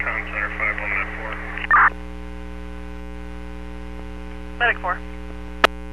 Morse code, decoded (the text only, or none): none